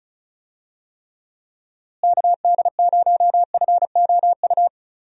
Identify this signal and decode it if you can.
Morse code, heard as a keyed tone, about 36 wpm, 700 Hz. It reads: KD0FOU
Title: USA Call Signs - 6 Characters - Repeated 2x - 35wpm